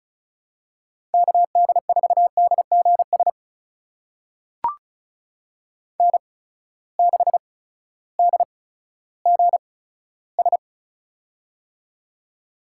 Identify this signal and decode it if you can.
Morse code, heard as a keyed tone, about 36 wpm, spaced as for 16 wpm, 700 Hz. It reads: KD4DGS N6DGS
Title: USA Call Signs - 8x Character Spacing - 35wpm